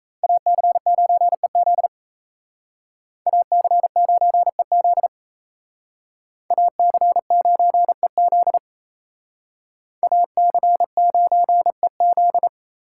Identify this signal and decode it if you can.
Morse code, heard as a keyed tone, about 33 wpm, 700 Hz. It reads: AC9EZ AC9EZ AC9EZ AC9EZ
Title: Speed Racing - USA Call Signs 28wpm